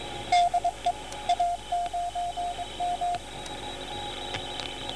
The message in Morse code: DE A9M